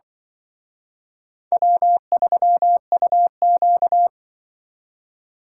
Morse code, heard as a keyed tone, about 25 wpm, 700 Hz. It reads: W3UQ
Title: Speed Racing - USA 1x2 Call Signs 20wpm